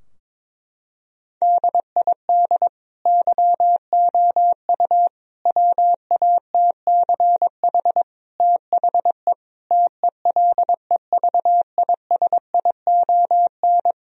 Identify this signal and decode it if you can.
Morse code, heard as a keyed tone, about 22 wpm, 700 Hz. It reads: DID YOU WATCH THE TELEVISION
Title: Sentences from Top 1000 Words - Review Part 4 - 22wpm